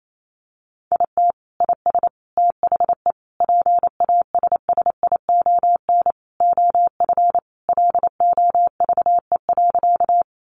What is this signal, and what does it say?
Morse code, heard as a keyed tone, about 28 wpm, 700 Hz. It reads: IT IS THE PASSION OF LOVE.
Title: Sentences from Top 1300 Words 28wpm